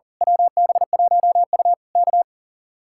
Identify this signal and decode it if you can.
Morse code, heard as a keyed tone, about 38 wpm, 700 Hz. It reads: WB1U K
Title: Speed-Racing Common QSO Phrases - 40wpm